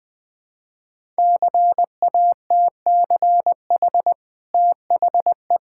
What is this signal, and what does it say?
Morse code, heard as a keyed tone, about 20 wpm, 700 Hz. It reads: CATCH THE
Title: Easy Sentences with Hints - 20wpm